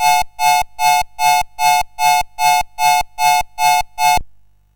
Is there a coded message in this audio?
No.